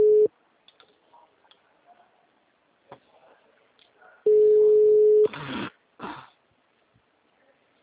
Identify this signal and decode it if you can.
Ringback tone (Europe, 425 Hz).